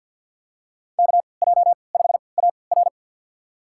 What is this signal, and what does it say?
Morse code, heard as a keyed tone, about 48 wpm, spaced as for 28 wpm, 700 Hz. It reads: KJ5AR
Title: USA Call Signs - 5 Characters - 3x Character Spacing - 50wpm